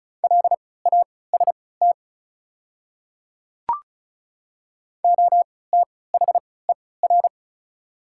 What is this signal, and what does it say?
Morse code, heard as a keyed tone, about 36 wpm, spaced as for 20 wpm, 690 Hz. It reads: LAST OTHER
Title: 3x Character Spacing - Top 300 Words - 35wpm